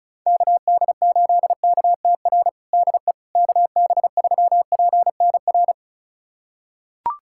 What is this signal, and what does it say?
Morse code, heard as a keyed tone, about 34 wpm, 700 Hz. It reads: KD8KTR DE KB3PNR
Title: USA Callsign DE Callsign - 6 Characters - 35wpm